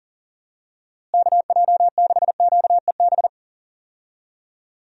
Morse code, heard as a keyed tone, about 37 wpm, 700 Hz. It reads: KJ6QEB
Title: USA Call Signs - 6 Characters - Repeated 2x - 40wpm